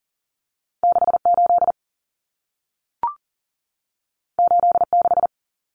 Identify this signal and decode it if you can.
Morse code, heard as a keyed tone, about 40 wpm, 700 Hz. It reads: 68 86
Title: Common Cut Numbers - 2-Digit 40wpm